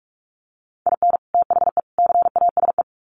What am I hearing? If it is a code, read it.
Morse code, heard as a keyed tone, about 44 wpm, 700 Hz. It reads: IN THE CASE
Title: Sets of 3 Words 45wpm